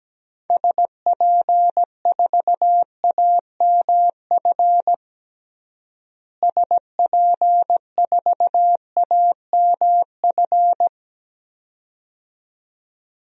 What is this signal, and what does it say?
Morse code, heard as a keyed tone, about 17 wpm, 700 Hz. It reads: SP4AMF SP4AMF
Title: DXCC-100 Call Signs - 6 Characters - Repeated 2x - 17wpm